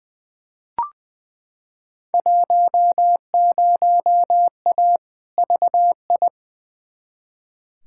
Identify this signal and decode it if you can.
Morse code, heard as a keyed tone, about 20 wpm, 700 Hz. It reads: 10A VI
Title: Field Day Exchanges Encore - 20wpm